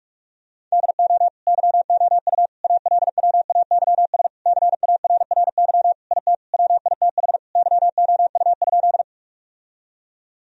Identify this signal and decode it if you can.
Morse code, heard as a keyed tone, about 46 wpm, 700 Hz. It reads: DO YOU ALWAYS CARRY IT WITH YOU?
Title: Sentences from Top 1000 Words - Review Part 1 - 45wpm